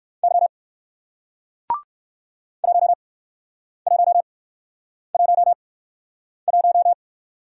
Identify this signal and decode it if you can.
Morse code, heard as a keyed tone, about 57 wpm, spaced as for 9 wpm, 700 Hz. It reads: Y 1111